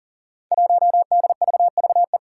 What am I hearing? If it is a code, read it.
Morse code, heard as a keyed tone, about 40 wpm, 700 Hz. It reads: 1DVVE
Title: Five Challenging Letter-Number Combinations 40wpm